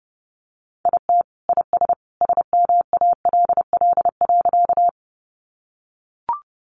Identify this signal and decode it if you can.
Morse code, heard as a keyed tone, about 30 wpm, 700 Hz. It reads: IT IS SMALL.
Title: Additional Easy Phrases 30wpm